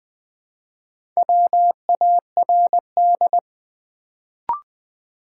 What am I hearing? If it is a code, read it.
Morse code, heard as a keyed tone, about 20 wpm, 700 Hz. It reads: WARD
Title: Build A Word - 1 Letter At A Time - 3 to 5-Letter Words - 20wpm